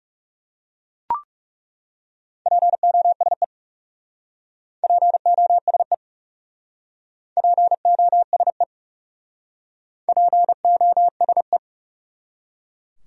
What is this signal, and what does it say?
Morse code, heard as a keyed tone, about 37 wpm, 700 Hz. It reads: POSE POSE POSE POSE
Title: Speed-Racing Top 500 Words 30wpm